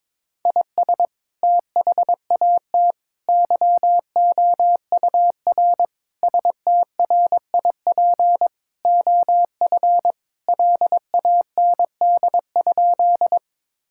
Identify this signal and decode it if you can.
Morse code, heard as a keyed tone, about 21 wpm, 700 Hz. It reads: IS THAT YOUR STRIP OF LAND?
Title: Sentences from Top 1600 Words 22wpm